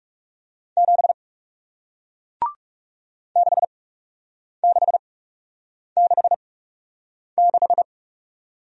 Morse code, heard as a keyed tone, about 39 wpm, spaced as for 8 wpm, 700 Hz. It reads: 7 6666